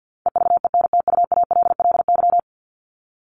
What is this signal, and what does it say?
Morse code, heard as a keyed tone, about 50 wpm, 700 Hz. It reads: EVENTUALLY
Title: LY Suffix - Top 100 Words with Base Words 50wpm